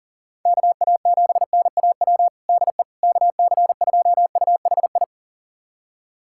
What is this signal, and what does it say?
Morse code, heard as a keyed tone, about 40 wpm, 700 Hz. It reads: KA7NAW DE KC2UHI
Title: USA Callsign DE Callsign 40wpm